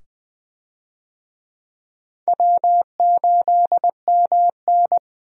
Morse code, heard as a keyed tone, about 20 wpm, 700 Hz. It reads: W8MN
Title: USA Call Signs - 4 Characters - 20wpm